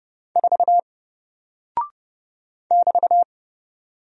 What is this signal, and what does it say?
Morse code, heard as a keyed tone, about 31 wpm, 700 Hz. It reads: 4 <BT>
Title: Single Character (Letter:Number:Punctuation:Prosign) 30wpm